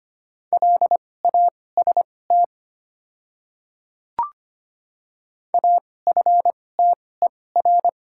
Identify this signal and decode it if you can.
Morse code, heard as a keyed tone, about 25 wpm, spaced as for 18 wpm, 700 Hz. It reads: LAST AFTER
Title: 2x Character Spacing - Top 300 Words - 25wpm